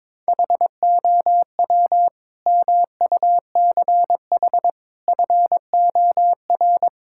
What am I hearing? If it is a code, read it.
Morse code, heard as a keyed tone, about 22 wpm, 700 Hz. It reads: HOW MUCH FOR
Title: Sentences from Top 1000 Words - Review Part 5 - 22wpm